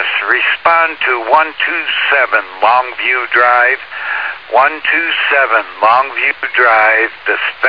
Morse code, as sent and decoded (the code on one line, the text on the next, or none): none